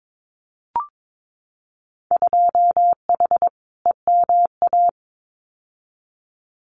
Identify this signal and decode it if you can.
Morse code, heard as a keyed tone, about 22 wpm, 700 Hz. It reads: 2H EMA
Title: Winter Field Day Exchanges 22wpm